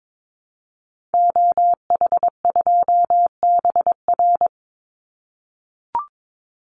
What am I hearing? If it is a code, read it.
Morse code, heard as a keyed tone, about 22 wpm, 700 Hz. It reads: OH2BR
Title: Call Signs - 22wpm